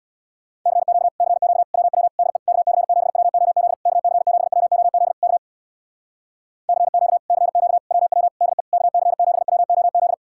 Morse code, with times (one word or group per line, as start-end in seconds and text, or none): 0.7-1.1 CQ
1.2-1.6 CQ
1.7-2.1 WW
2.2-2.4 DE
2.5-3.7 KC7WOZ
3.9-5.1 KC7WOZ
5.2-5.4 K
6.7-7.2 CQ
7.3-7.8 CQ
7.9-8.3 WW
8.4-8.6 DE
8.7-10.1 KC7WOZ